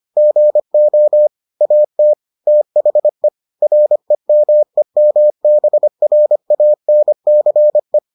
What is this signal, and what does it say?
Morse code, heard as a keyed tone, about 25 wpm, 600 Hz. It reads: GO AT THE REMEMBRANCE